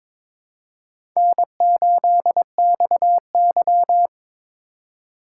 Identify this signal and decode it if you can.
Morse code, heard as a keyed tone, about 22 wpm, 700 Hz. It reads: N8XY
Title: USA Call Signs - 4 Characters - 2x Character Spacing - 22wpm